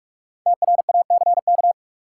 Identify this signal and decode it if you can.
Morse code, heard as a keyed tone, about 48 wpm, 700 Hz. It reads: TRACK
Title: Five Letter Words - 45wpm